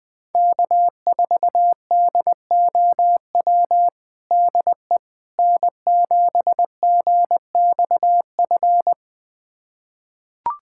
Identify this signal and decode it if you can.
Morse code, heard as a keyed tone, about 20 wpm, 700 Hz. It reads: K4DOW DE N7GXF
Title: USA Callsign DE Callsign - 5 Characters - 20wpm